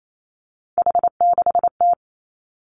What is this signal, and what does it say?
Morse code, heard as a keyed tone, about 28 wpm, 690 Hz. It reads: H6T
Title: Three Challenging Letter-Number Combinations 28wpm